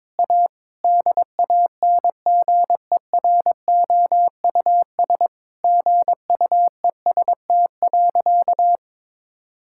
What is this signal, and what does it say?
Morse code, heard as a keyed tone, about 22 wpm, 700 Hz. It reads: A DANGEROUS GUEST.
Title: Sentences from Top 1600 Words 22wpm